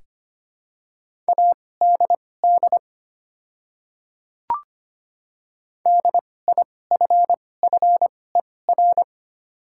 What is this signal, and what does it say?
Morse code, heard as a keyed tone, about 25 wpm, spaced as for 18 wpm, 690 Hz. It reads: ADD DIFFER
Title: 2x Character Spacing - Top 300 Words - 25wpm